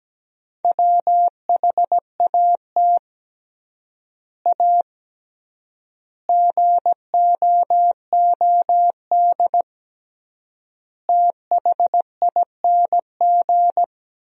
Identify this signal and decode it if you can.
Morse code, heard as a keyed tone, about 17 wpm, 700 Hz. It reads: WHAT A GOOD THING
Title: 3x Word-Spacing - Sentences from Top 200 Words - 17wpm